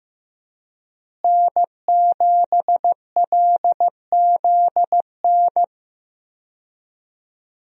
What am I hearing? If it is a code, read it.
Morse code, heard as a keyed tone, about 15 wpm, 700 Hz. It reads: N7LZN
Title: USA Call Signs - 5 Characters - Repeated 2x - 15wpm